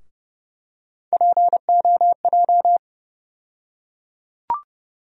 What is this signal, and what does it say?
Morse code, heard as a keyed tone, about 31 wpm, 700 Hz. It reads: POJ
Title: USA Call Sign Postfixes - 3 Letters 30wpm